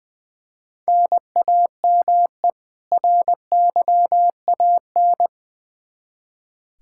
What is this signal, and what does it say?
Morse code, heard as a keyed tone, about 20 wpm, 700 Hz. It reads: NAME RYAN
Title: Practice QSO - Exchange 1 - Beginning - 20wpm